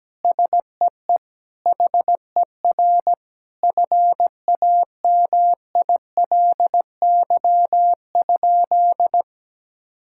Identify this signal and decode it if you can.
Morse code, heard as a keyed tone, about 17 wpm, 700 Hz. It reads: SEE HER FAMILY?
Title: Sentences from Top 1000 Words - Review Part 6 - 17wpm